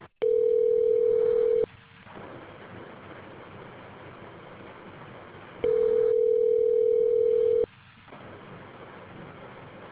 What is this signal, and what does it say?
Ringback tone (North America), 440 + 480 Hz.